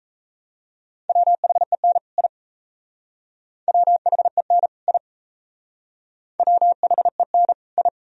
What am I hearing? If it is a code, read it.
Morse code, heard as a keyed tone, about 37 wpm, 700 Hz. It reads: WHEN I WHEN I WHEN I